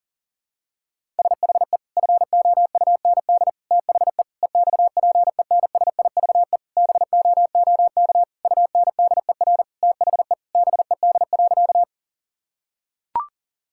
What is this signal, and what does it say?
Morse code, heard as a keyed tone, about 40 wpm, 700 Hz. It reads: SHE FOUND THE EXPENSIVE BOOK UNDER THE BED.
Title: Sentences from Top 1000 Words - Review Part 6 - 40wpm